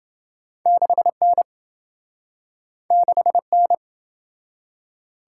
Morse code, heard as a keyed tone, about 27 wpm, 700 Hz. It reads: 6N 6N